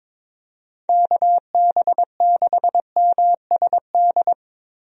Morse code, heard as a keyed tone, about 22 wpm, 700 Hz. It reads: KB6MSD
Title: USA Call Signs - 6 Characters - Repeated 3x - 22wpm